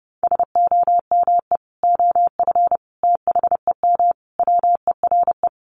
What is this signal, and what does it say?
Morse code, heard as a keyed tone, about 30 wpm, 700 Hz. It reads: SOME OF THEM WERE
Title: Sentences from Top 1200 Words 30wpm